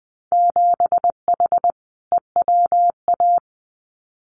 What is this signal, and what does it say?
Morse code, heard as a keyed tone, about 20 wpm, 700 Hz. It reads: 7H EWA
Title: Winter Field Day Exchanges Encore 20wpm